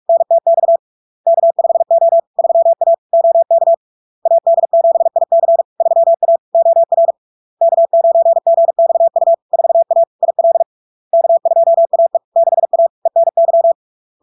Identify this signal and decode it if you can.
Morse code, heard as a keyed tone, about 46 wpm, 670 Hz. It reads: NTX K5O 3A OK AD7IC 3A OR K9GXU 4A IL K2AE 6A ENY
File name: LARK-FD-2022-Log-in-CW-45wpm-670-Hz